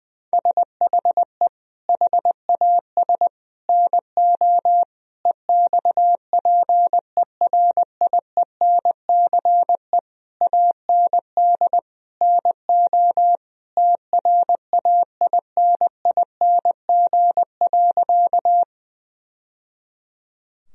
Morse code, heard as a keyed tone, about 21 wpm, 700 Hz. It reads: SHE HAS NO EXPERIENCE AND NO TRAINING.
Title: Sentences from Top 1000 Words - Review Part 4 - 20wpm